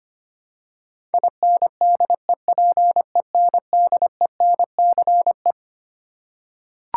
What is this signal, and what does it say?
Morse code, heard as a keyed tone, about 25 wpm, 700 Hz. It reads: INDEPENDENCE
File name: Twelve Letter Words - 25wpm